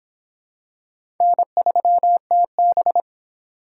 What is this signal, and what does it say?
Morse code, heard as a keyed tone, about 25 wpm, 700 Hz. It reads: N3TB